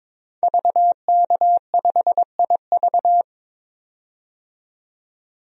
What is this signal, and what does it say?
Morse code, heard as a keyed tone, about 22 wpm, 700 Hz. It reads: VK5IV
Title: DXCC-100 Call Signs - 5 Characters - 22wpm